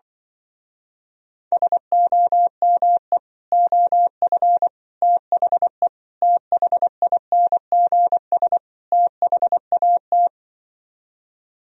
Morse code, heard as a keyed tone, about 24 wpm, 700 Hz. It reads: SOME OF THE THINGS THAT
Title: Speed-Racing - Sets of 5 Words 20wpm